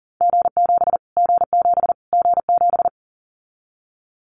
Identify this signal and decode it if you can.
Morse code, heard as a keyed tone, about 38 wpm, 700 Hz. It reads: G7 G7 G7
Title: Callsign Prefixes for Top 25 DXCC - Repeated 3x - 40wpm